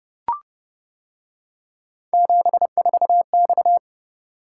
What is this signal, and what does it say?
Morse code, heard as a keyed tone, about 28 wpm, 700 Hz. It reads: 74X